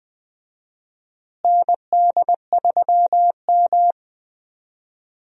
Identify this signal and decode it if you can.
Morse code, heard as a keyed tone, about 20 wpm, 700 Hz. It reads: ND3M
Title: USA Call Signs - 4 Characters - Repeated 3x - 20wpm